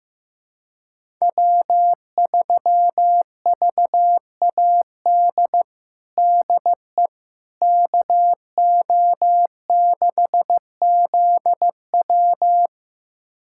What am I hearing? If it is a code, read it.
Morse code, heard as a keyed tone, about 15 wpm, 700 Hz. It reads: W3VAD DE KO6ZW
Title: USA Callsign DE Callsign - 5 Characters - 15wpm